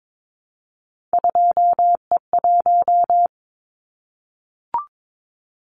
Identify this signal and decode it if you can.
Morse code, heard as a keyed tone, about 22 wpm, 700 Hz. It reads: 2E1
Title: Callsign Prefixes for Top 25 DXCC - Repeated 3x - 22wpm